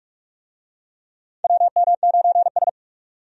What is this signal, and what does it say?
Morse code, heard as a keyed tone, about 46 wpm, 700 Hz. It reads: WM9S
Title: USA Call Signs - 4 Characters - 45wpm